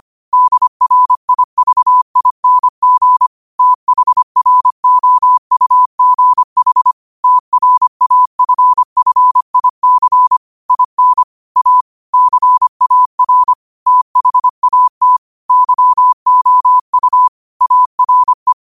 Morse code, heard as a keyed tone, about 25 wpm, 1000 Hz. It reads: DRIVING THROUGH TRAFFIC IN A CAR THAT YOU ARE
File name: qotdfiles / 25wpm